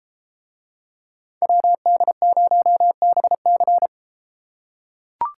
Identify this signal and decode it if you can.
Morse code, heard as a keyed tone, about 33 wpm, 700 Hz. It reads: WD0BC